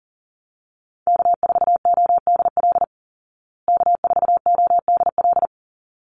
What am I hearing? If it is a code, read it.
Morse code, heard as a keyed tone, about 40 wpm, 700 Hz. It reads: K4ODL K4ODL
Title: Call Signs Repeated 3x - 40wpm